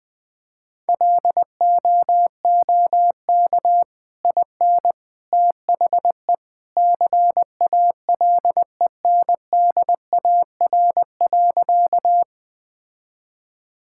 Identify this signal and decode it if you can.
Morse code, heard as a keyed tone, about 21 wpm, 700 Hz. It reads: LOOK IN THE CALENDAR.
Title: Sentences from Top 1600 Words 20wpm